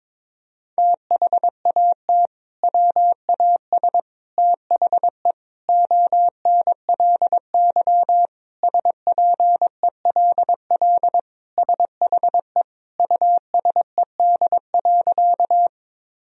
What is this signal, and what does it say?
Morse code, heard as a keyed tone, about 22 wpm, 700 Hz. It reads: THAT WAS THE ONLY SPELL SHE USED.
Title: Sentences from Top 1000 Words - Review Part 1 - 22wpm